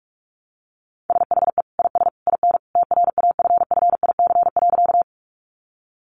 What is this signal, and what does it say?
Morse code, heard as a keyed tone, about 44 wpm, 700 Hz. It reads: SHE IS IN TRAFFIC.
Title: Sentences from Top 1300 Words 45wpm